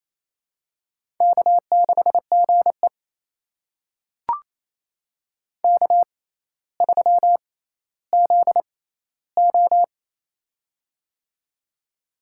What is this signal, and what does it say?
Morse code, heard as a keyed tone, about 28 wpm, spaced as for 14 wpm, 700 Hz. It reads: K6GE K3ZO
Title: USA Call Signs - 4 Characters - 6x Character Spacing - 28wpm